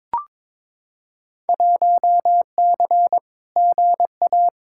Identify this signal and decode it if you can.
Morse code, heard as a keyed tone, about 23 wpm, 700 Hz. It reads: 1C GA
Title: Field Day Exchanges - 22wpm